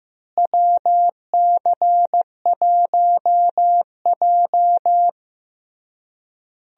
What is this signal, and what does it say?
Morse code, heard as a keyed tone, about 16 wpm, 700 Hz. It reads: WC1J
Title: USA Call Signs - 4 Characters - Repeated 3x - 15wpm